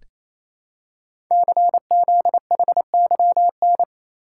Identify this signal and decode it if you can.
Morse code, heard as a keyed tone, about 27 wpm, 700 Hz. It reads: CZHYN
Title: Code Groups - 5 Letters - 28wpm